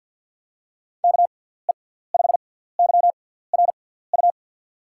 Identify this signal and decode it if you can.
Morse code, heard as a keyed tone, about 53 wpm, spaced as for 17 wpm, 700 Hz. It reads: KE5YRU
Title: USA Call Signs - 6x Character Spacing - 50wpm